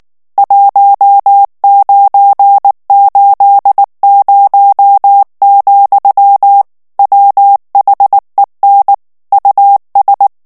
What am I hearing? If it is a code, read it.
Morse code, heard as a keyed tone, about 19 wpm, 800 Hz. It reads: 1980, WHEN US
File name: ANGLAIS texte clair 3 vitesse 18 mots minute